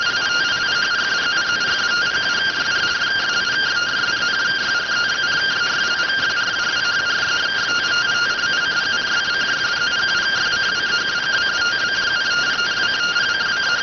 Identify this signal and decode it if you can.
Two-tone teleprinter signal, about 170 Hz shift.